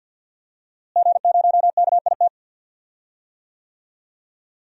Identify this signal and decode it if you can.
Morse code, heard as a keyed tone, about 46 wpm, 700 Hz. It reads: G0KIT
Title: DXCC-100 Call Signs - 5 Characters - Repeated 2x - 50wpm